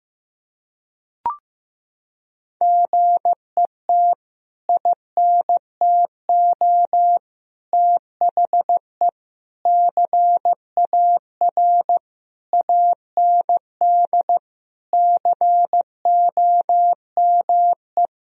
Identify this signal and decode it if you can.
Morse code, heard as a keyed tone, about 15 wpm, 700 Hz. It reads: GET INTO THE CAR AND COME
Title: Sentences from Top 1000 Words - Review Part 1 - 15wpm